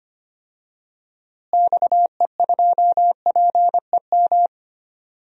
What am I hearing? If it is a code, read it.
Morse code, heard as a keyed tone, about 25 wpm, 700 Hz. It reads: XE2PEM
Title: DXCC-100 Call Signs - 6 Characters - Repeated 3x - 25wpm